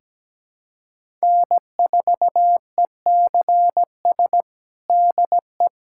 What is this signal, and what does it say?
Morse code, heard as a keyed tone, about 17 wpm, 700 Hz. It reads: N4ECS DE
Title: USA Callsign DE Callsign - 5 Characters - 17wpm